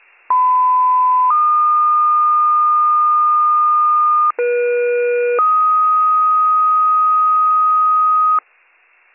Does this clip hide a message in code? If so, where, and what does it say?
Yes: two-tone page from 0.3 to 8.4 s, heard as a one-second tone, then a three-second tone, about 1010 then 1220 Hz, about 480 then 1250 Hz.